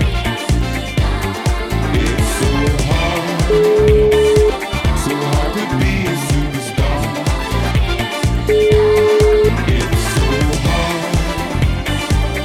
Telephone call progress tone (Europe, 425 Hz), heard as a telephone network's tone, about 420 Hz, with noise. Ringback tone.